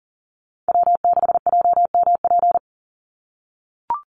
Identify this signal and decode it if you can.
Morse code, heard as a keyed tone, about 40 wpm, 700 Hz. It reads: W6JMP
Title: NA SOTA Chaser Callsigns 40wpm